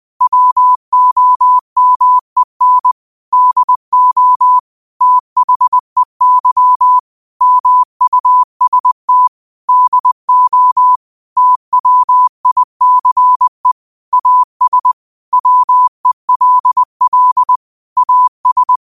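Morse code, heard as a keyed tone, about 20 wpm, 1000 Hz. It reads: WOMEN DO THEY MUST DO TWICE AS WELL AS
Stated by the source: Morse quote practice at 20 wpm.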